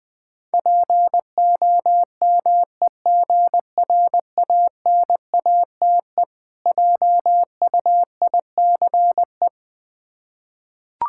Morse code, heard as a keyed tone, about 20 wpm, 700 Hz. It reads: POMEGRANATE JUICE
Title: Difficult Words with Hints - 20wpm